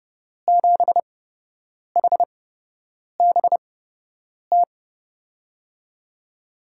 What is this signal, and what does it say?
Morse code, heard as a keyed tone, about 30 wpm, spaced as for 8 wpm, 700 Hz. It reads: 7HBT
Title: USA Call Signs - 6 Characters - 8x Character Spacing - 30wpm